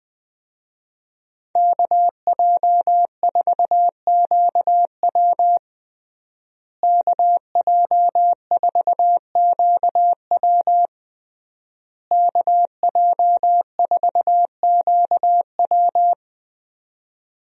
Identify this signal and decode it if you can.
Morse code, heard as a keyed tone, about 20 wpm, 700 Hz. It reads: KJ4QW KJ4QW KJ4QW
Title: USA Call Signs - 5 Characters - Repeated 3x - 20wpm